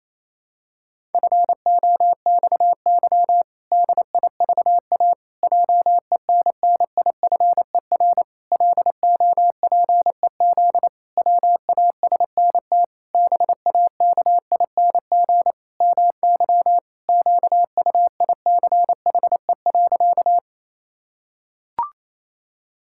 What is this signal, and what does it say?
Morse code, heard as a keyed tone, about 28 wpm, 700 Hz. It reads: FOXY DIVA JENNIFER LOPEZ WASNT BAKING MY QUICHE.
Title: Pangram Challenge v2 - 28wpm